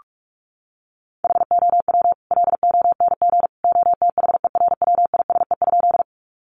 Morse code, heard as a keyed tone, about 44 wpm, 700 Hz. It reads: HOW LONG OTHERWISE?
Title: Sentences from Top 1200 Words 45wpm